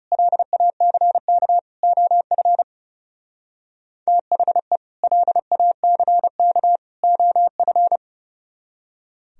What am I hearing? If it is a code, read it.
Morse code, heard as a keyed tone, about 36 wpm, 700 Hz. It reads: LACK OF THE LACK OF
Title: Speed-Racing - Sets of 3 Words 30wpm